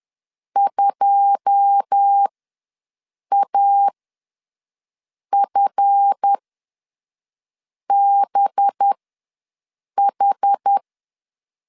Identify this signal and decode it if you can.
Morse code, heard as a keyed tone, about 10 wpm, spaced as for 5 wpm, 800 Hz. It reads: 2AFBH
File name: LEZIONI / 9 2867 ws 10wpm 4215